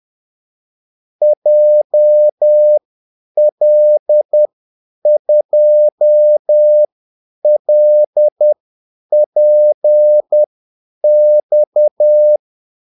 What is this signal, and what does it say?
Morse code, heard as a keyed tone, about 10 wpm, spaced as for 8 wpm, 600 Hz. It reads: JL2LPX
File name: Callsigns-10-8wpm